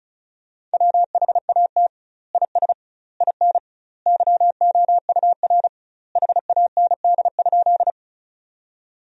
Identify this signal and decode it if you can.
Morse code, heard as a keyed tone, about 36 wpm, 700 Hz. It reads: WHAT IS IN YOUR HAND?
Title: 2x Word-Spacing - Sentences from Top 200 Words - 35wpm